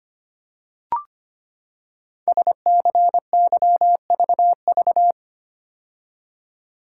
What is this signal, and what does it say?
Morse code, heard as a keyed tone, about 25 wpm, 700 Hz. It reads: SCYVV
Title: Code Groups - 5 Letters - 25wpm